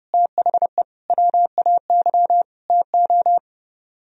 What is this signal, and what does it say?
Morse code, heard as a keyed tone, about 31 wpm, 700 Hz. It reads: THE WAY TO